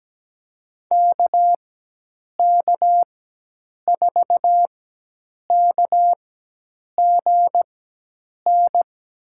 Morse code, heard as a keyed tone, about 17 wpm, spaced as for 8 wpm, 700 Hz. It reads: KK4KGN
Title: USA Call Signs - 6 Characters - 4x Character Spacing - 17wpm